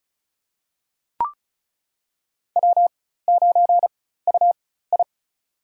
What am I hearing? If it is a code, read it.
Morse code, heard as a keyed tone, about 36 wpm, spaced as for 16 wpm, 700 Hz. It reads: W9UI
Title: USA Call Signs - 4x Character Spacing - 35wpm